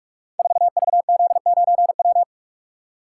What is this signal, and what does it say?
Morse code, heard as a keyed tone, about 44 wpm, 700 Hz. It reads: 4VZ9W
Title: Code Groups - 5 Letter:Numbers - 45wpm